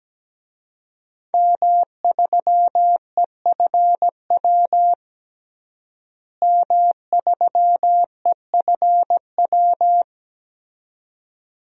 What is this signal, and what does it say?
Morse code, heard as a keyed tone, about 17 wpm, 700 Hz. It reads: M3EFW M3EFW
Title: DXCC-100 Call Signs - 5 Characters - Repeated 2x - 17wpm